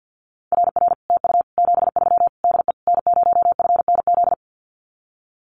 Morse code, heard as a keyed tone, about 50 wpm, 700 Hz. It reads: RR TU 73 DE N0FNZ
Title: Common QSO Phrases Encore 50wpm